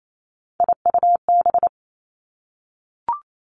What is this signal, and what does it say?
Morse code, heard as a keyed tone, about 28 wpm, 700 Hz. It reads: IUB
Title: Three Challenging Letter-Number Combinations 28wpm